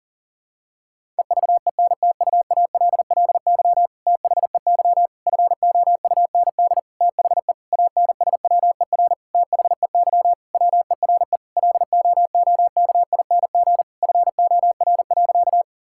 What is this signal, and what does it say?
Morse code, heard as a keyed tone, about 40 wpm, 700 Hz. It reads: EVENTUALLY THEY FOUND THE ANSWER THEY WERE LOOKING FOR.
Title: Sentences from Top 1000 Words - Review Part 6 - 40wpm